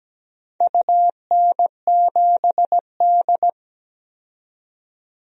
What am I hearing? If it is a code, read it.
Morse code, heard as a keyed tone, about 17 wpm, 700 Hz. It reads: UN7D
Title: DXCC-100 Call Signs - 4 Characters - 17wpm